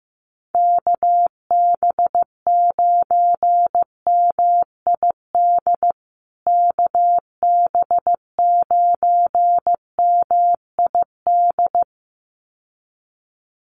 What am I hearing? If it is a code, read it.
Morse code, heard as a keyed tone, about 15 wpm, 700 Hz. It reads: KB9MID KB9MID
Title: USA Call Signs Repeated 2x 15wpm